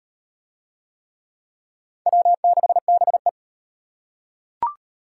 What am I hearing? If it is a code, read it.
Morse code, heard as a keyed tone, about 33 wpm, 700 Hz. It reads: W6BE